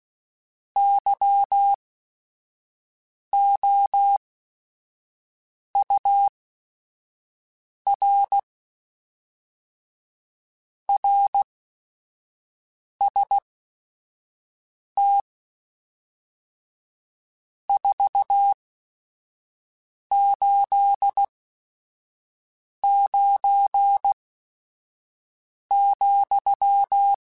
Morse code, heard as a keyed tone, about 16 wpm, spaced as for 5 wpm, 800 Hz. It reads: YOUR RST 489,
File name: RSTs / RST-mp3s / Your RSTs @5